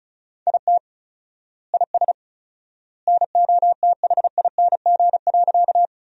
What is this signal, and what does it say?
Morse code, heard as a keyed tone, about 34 wpm, 700 Hz. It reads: IT IS NOTHING.
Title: 4x Word-Spacing - Sentences from Top 500 Words - 35wpm